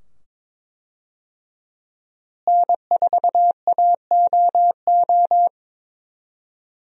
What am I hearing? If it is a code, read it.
Morse code, heard as a keyed tone, about 22 wpm, 700 Hz. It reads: N4AOO
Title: USA Call Signs - 5 Characters - Repeated 2x - 22wpm